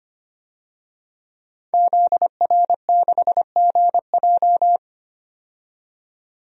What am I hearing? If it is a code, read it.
Morse code, heard as a keyed tone, about 25 wpm, 700 Hz. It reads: ZR6GJ
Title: DXCC-100 Call Signs - 5 Characters - 25wpm